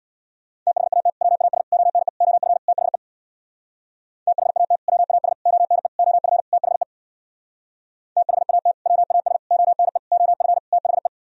Morse code, heard as a keyed tone, about 64 wpm, 700 Hz. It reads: THAT WAS ONE OF THE THAT WAS ONE OF THE THAT WAS ONE OF THE